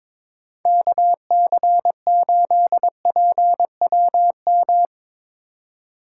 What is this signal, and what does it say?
Morse code, heard as a keyed tone, about 23 wpm, 700 Hz. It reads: KC8PWM
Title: Speed Racing - USA 2x3 Call Signs 22wpm